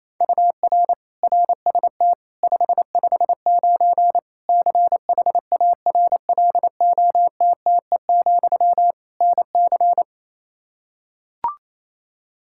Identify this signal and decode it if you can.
Morse code, heard as a keyed tone, about 28 wpm, 700 Hz. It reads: UR RST 559 CHARLOTTE, NC
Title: Practice QSO - Exchange 1 - Beginning - 28wpm